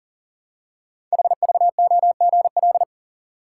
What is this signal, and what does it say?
Morse code, heard as a keyed tone, about 39 wpm, 700 Hz. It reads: HVOGL
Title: Code Groups - 5 Letter:Numbers - 40wpm